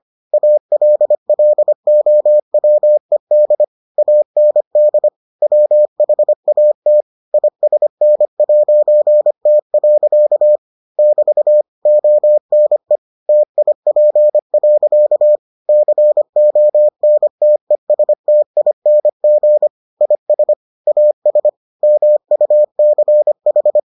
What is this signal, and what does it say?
Morse code, heard as a keyed tone, about 25 wpm, 600 Hz. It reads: ALLOWED AND WHAT ISN'T. <BT> ONE TIP. CONTESTING IS AS MUCH